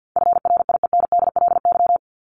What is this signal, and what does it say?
Morse code, heard as a keyed tone, about 50 wpm, 700 Hz. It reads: FRIENDLY
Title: Top 1751-2000 Words 50wpm